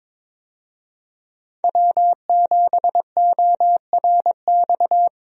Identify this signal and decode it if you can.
Morse code, heard as a keyed tone, about 22 wpm, 700 Hz. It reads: W7ORX
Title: USA Call Signs - 5 Characters - Repeated 2x - 22wpm